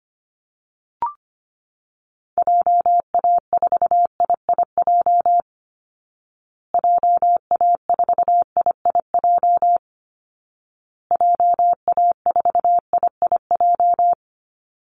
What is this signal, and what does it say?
Morse code, heard as a keyed tone, about 25 wpm, 700 Hz. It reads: JA4IIJ JA4IIJ JA4IIJ
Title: Call Signs Repeated 2x - 25wpm